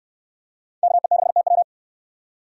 Morse code, heard as a keyed tone, about 68 wpm, 700 Hz. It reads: KE6TW